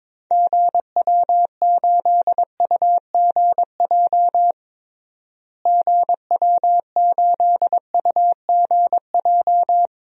Morse code, heard as a keyed tone, about 22 wpm, 700 Hz. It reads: GW8UGJ GW8UGJ
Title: DXCC-100 Call Signs - 6 Characters - Repeated 3x - 22wpm